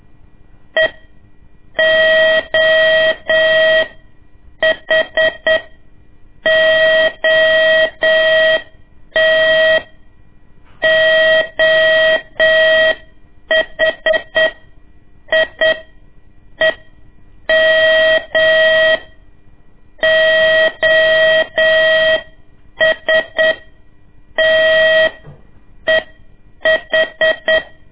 Morse code, heard as a keyed tone, about 8 wpm, spaced as for 6 wpm, 640 Hz. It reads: EOHOT OHIEM OSTEH
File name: ik1weg2 / CORSI CW / lez2-1